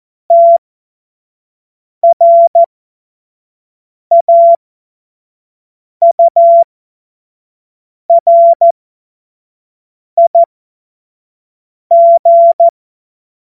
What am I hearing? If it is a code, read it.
Morse code, heard as a keyed tone, about 14 wpm, spaced as for 5 wpm, 680 Hz. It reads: TRAURIG
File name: Morsekurs / Witze / Witz 11. Mai 2021, 25 F70